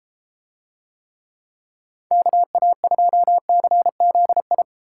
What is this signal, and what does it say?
Morse code, heard as a keyed tone, about 31 wpm, 700 Hz. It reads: KA2CZI